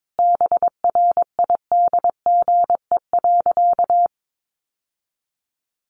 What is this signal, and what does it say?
Morse code, heard as a keyed tone, about 22 wpm, 700 Hz. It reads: BRIDGE.
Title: Sentences from Top 1500 Words 22wpm